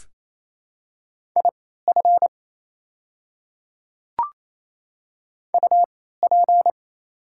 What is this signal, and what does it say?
Morse code, heard as a keyed tone, about 27 wpm, spaced as for 16 wpm, 690 Hz. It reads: IF UP
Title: 3x Character Spacing - Top 300 Words - 28wpm